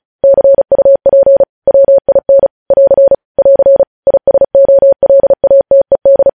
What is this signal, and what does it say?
Morse code, heard as a keyed tone, about 36 wpm, 550 Hz. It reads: CUP WIN <AR> <AR> ISOLATED
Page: News headlines in Morse code at 35 wpm